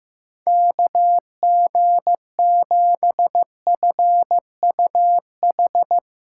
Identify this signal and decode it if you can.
Morse code, heard as a keyed tone, about 15 wpm, 700 Hz. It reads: KG7FUH
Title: USA Call Signs - 6 Characters - 3x Character Spacing - 15wpm